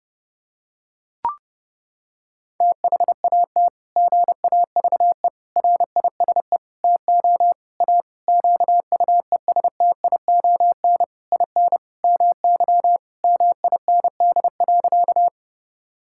Morse code, heard as a keyed tone, about 30 wpm, 700 Hz. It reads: THAT GAVE RISE TO A QUESTION IN MY MIND.
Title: Sentences from Top 1000 Words - Review Part 6 - 30wpm